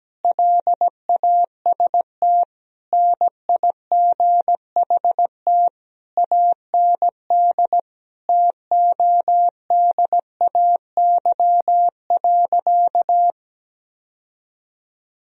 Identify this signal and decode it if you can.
Morse code, heard as a keyed tone, about 17 wpm, 700 Hz. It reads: LAST NIGHT AND TODAY.
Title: Sentences from Top 1000 Words - Review Part 4 - 17wpm